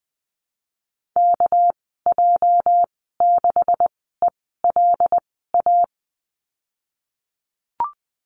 Morse code, heard as a keyed tone, about 20 wpm, spaced as for 15 wpm, 700 Hz. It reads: KJ6ELA
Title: 238 Lesson 38g - Letter q - Call Signs 20wpm